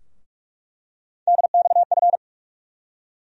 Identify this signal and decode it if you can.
Morse code, heard as a keyed tone, about 44 wpm, 700 Hz. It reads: DXF
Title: USA Call Sign Postfixes - 3 Letters 45wpm